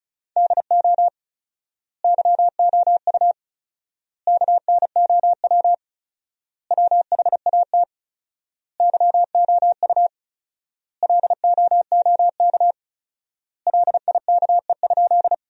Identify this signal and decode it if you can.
Morse code, heard as a keyed tone, about 36 wpm, 700 Hz. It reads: DO YOU KNOW WHAT YOU LOOK LIKE?
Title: 4x Word-Spacing - Sentences from Top 100 Words - 35wpm